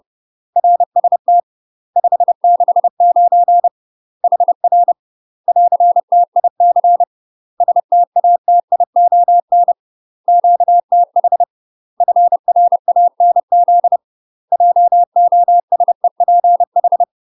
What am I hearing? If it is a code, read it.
Morse code, heard as a keyed tone, about 30 wpm, 700 Hz. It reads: RST 569 HR <AR>TIC STATION QTH FRANZ JOSEPH
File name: UK MASTER QSOs TRAINING / 30WPM / UK MASTER QSOs 0026